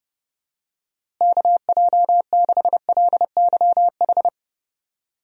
Morse code, heard as a keyed tone, about 32 wpm, 700 Hz. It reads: KJ6LYH
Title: USA Call Signs - 6 Characters - Repeated 2x - 30wpm